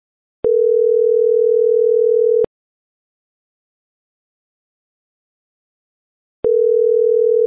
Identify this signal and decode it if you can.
Ringback tone (North America), 440 + 480 Hz.